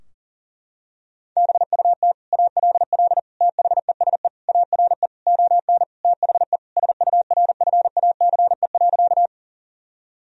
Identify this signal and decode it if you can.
Morse code, heard as a keyed tone, about 37 wpm, 700 Hz. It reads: BUT ALL THESE ARE ON THE SURFACE.
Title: Sentences from Top 1000 Words - Review Part 3 - 40wpm